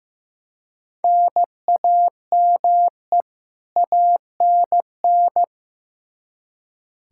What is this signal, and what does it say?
Morse code, heard as a keyed tone, about 16 wpm, 700 Hz. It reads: NAME ANN
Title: K1USN SST QSO Practice 15wpm